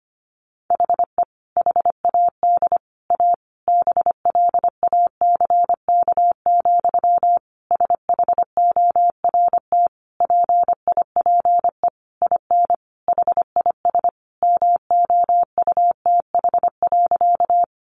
Morse code, heard as a keyed tone, about 25 wpm, 700 Hz. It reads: HE HAD A BLACK, SHORT PIPE IN HIS MOUTH.
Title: Sentences from Top 1500 Words 25wpm